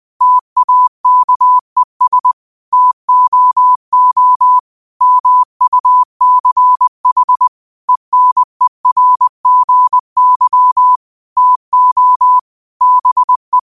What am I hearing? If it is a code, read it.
Morse code, heard as a keyed tone, about 20 wpm, 1000 Hz. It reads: TAKES TOO MUCH ENERGY TO BE